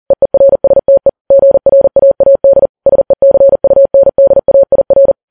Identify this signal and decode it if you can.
Morse code, heard as a keyed tone, about 40 wpm, 550 Hz. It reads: EERSTE GRAAD SECUNDAIR
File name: Files Web tls / Morse on9fv / morse40kps